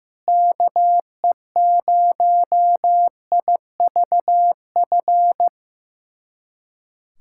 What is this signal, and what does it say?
Morse code, heard as a keyed tone, about 15 wpm, 700 Hz. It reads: KE0IVF
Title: USA Callsign DE Callsign - 6 Characters - 15wpm